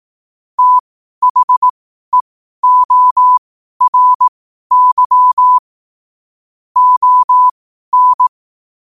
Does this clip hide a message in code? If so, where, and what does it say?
Yes: Morse code from 0.6 to 8.3 s, heard as a keyed tone, about 18 wpm, spaced as for 12 wpm, 1000 Hz. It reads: THEORY ON